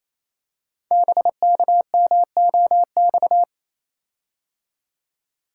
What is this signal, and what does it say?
Morse code, heard as a keyed tone, about 29 wpm, 700 Hz. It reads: BKMOX
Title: Code Groups - 5 Letters - 28wpm